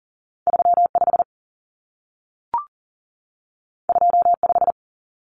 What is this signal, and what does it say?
Morse code, heard as a keyed tone, about 40 wpm, 700 Hz. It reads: 35 25
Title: Numbers - 2-Digit 40wpm